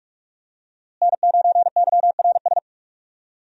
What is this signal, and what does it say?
Morse code, heard as a keyed tone, about 46 wpm, 700 Hz. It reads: N9YRS
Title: USA Call Signs - 5 Characters - Repeated 3x - 45wpm